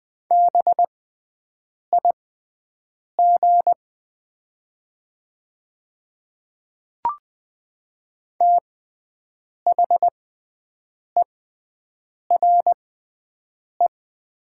Morse code, heard as a keyed tone, about 20 wpm, spaced as for 7 wpm, 700 Hz. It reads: BIG THERE
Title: 6x Character Spacing - Top 300 Words - 20wpm